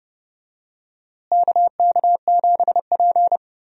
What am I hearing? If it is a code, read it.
Morse code, heard as a keyed tone, about 30 wpm, 700 Hz. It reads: KK7P
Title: USA Call Signs - 4 Characters - Repeated 2x - 30wpm